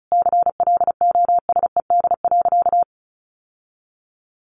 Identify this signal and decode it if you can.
Morse code, heard as a keyed tone, about 34 wpm, 700 Hz. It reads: CLOSED.
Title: Sentences from Top 1400 Words 35wpm